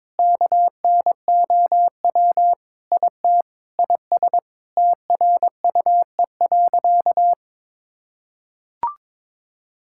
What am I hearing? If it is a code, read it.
Morse code, heard as a keyed tone, about 22 wpm, 700 Hz. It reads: KNOW IT IS TRUE.
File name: practice / 22wpm / Sentences with 4-Word Phrases - 22wpm